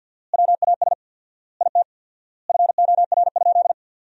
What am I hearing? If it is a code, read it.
Morse code, heard as a keyed tone, about 50 wpm, 700 Hz. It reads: WAS IT FOR?
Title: 4x Word-Spacing - Sentences from Top 100 Words - 50wpm